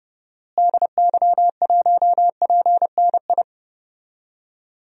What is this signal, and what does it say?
Morse code, heard as a keyed tone, about 30 wpm, 700 Hz. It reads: DY1PNI
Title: DXCC-100 Call Signs - 6 Characters - Repeated 2x - 30wpm